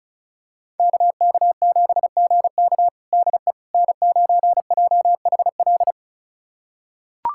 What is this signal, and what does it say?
Morse code, heard as a keyed tone, about 34 wpm, 700 Hz. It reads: KK7GK DE N9JHL
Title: USA Callsign DE Callsign - 5 Characters - 35wpm